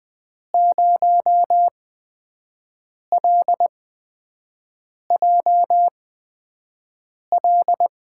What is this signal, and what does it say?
Morse code, heard as a keyed tone, about 20 wpm, spaced as for 5 wpm, 700 Hz. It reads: 0LJL
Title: USA Call Signs - 6 Characters - 8x Character Spacing - 20wpm